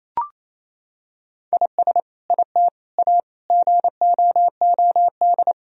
Morse code, heard as a keyed tone, about 28 wpm, 700 Hz. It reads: IS IT A GOOD
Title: Sentences from Top 1000 Words - Review Part 4 - 28wpm